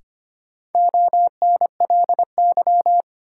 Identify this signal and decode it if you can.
Morse code, heard as a keyed tone, about 25 wpm, 700 Hz. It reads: ONLY
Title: Sentences from Top 200 Words - New Word Drill - 25wpm